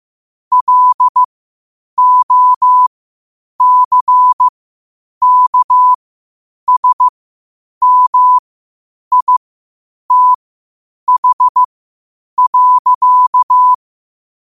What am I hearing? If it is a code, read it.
Morse code, heard as a keyed tone, about 15 wpm, spaced as for 8 wpm, 1000 Hz. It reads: LOCKSMITH.
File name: qotdfiles / 8wpm